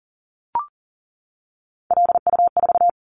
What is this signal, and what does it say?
Morse code, heard as a keyed tone, about 40 wpm, 700 Hz. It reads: LU4
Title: Callsign Prefixes for Top 50 DXCC - 40wpm